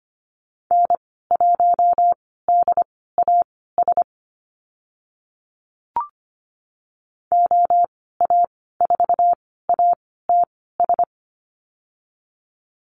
Morse code, heard as a keyed tone, about 25 wpm, spaced as for 16 wpm, 700 Hz. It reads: N1DAS OA4ATS
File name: Lesson 15g - Number 5 - Call Signs 25wpm